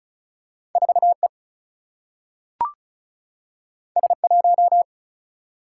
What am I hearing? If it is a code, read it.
Morse code, heard as a keyed tone, about 36 wpm, 700 Hz. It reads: 4E S1
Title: Single-Letter Single-Number 35wpm